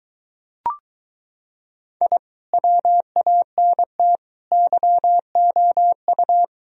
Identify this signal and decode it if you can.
Morse code, heard as a keyed tone, about 23 wpm, 700 Hz. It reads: I WANT YOU